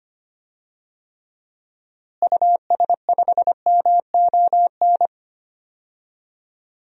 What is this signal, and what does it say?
Morse code, heard as a keyed tone, about 25 wpm, 700 Hz. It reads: US5MON
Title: DXCC-100 Call Signs - 6 Characters - Repeated 3x - 25wpm